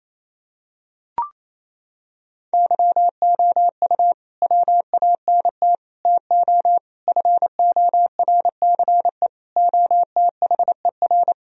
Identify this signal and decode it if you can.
Morse code, heard as a keyed tone, about 28 wpm, 700 Hz. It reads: YOU WANT TO FORCE OTHER
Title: Sentences from Top 1000 Words - Review Part 3 - 28wpm